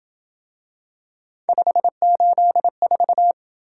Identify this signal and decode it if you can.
Morse code, heard as a keyed tone, about 27 wpm, 700 Hz. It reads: 584